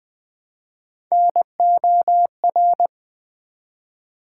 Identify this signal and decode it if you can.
Morse code, heard as a keyed tone, about 20 wpm, 700 Hz. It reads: NOR
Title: Build A Word - 1 Letter At A Time - 3 to 5-Letter Words - 20wpm